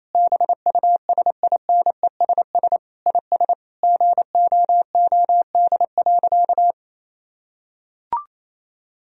Morse code, heard as a keyed tone, about 28 wpm, 700 Hz. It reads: BUSINESS IS GOOD.
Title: Sentences from Top 1000 Words - Review Part 4 - 28wpm